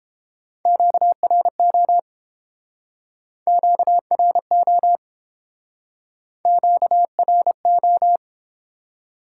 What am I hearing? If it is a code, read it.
Morse code, heard as a keyed tone, about 32 wpm, 700 Hz. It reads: QRO QRO QRO